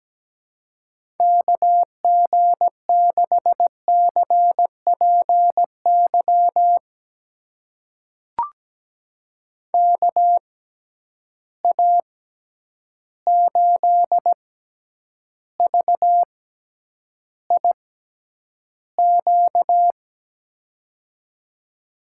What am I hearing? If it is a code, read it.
Morse code, heard as a keyed tone, about 17 wpm, spaced as for 9 wpm, 700 Hz. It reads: KG6CPY KA8VIQ
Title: USA Call Signs - 6 Characters - 6x Character Spacing - 17wpm